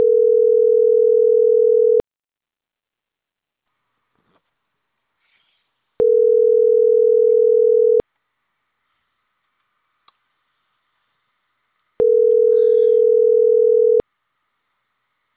Ringback tone (North America), 440 + 480 Hz.